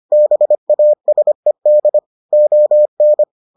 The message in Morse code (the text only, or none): BASED ON